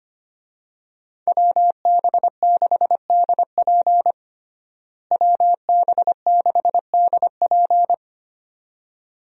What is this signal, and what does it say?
Morse code, heard as a keyed tone, about 25 wpm, 700 Hz. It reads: WB6DP WB6DP
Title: USA Call Signs - 5 Characters - Repeated 2x - 25wpm